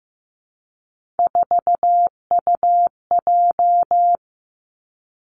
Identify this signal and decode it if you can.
Morse code, heard as a keyed tone, about 15 wpm, 700 Hz. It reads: 4UJ
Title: Three Challenging Letter-Number Combinations 15wpm